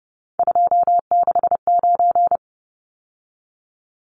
Morse code, heard as a keyed tone, about 30 wpm, 700 Hz. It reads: 269
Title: Numbers - 3-Digit 30wpm